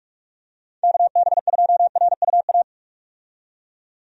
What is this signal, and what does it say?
Morse code, heard as a keyed tone, about 42 wpm, 700 Hz. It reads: KB2RUA
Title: USA Call Signs - 6 Characters - 45wpm